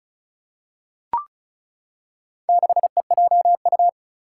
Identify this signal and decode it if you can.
Morse code, heard as a keyed tone, about 34 wpm, 700 Hz. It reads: 6EJU
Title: Four Challenging Letter-Number Combinations 35wpm